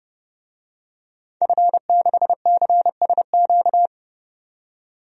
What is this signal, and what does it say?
Morse code, heard as a keyed tone, about 30 wpm, 700 Hz. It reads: F6CSQ
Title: DXCC-100 Call Signs - 5 Characters - 30wpm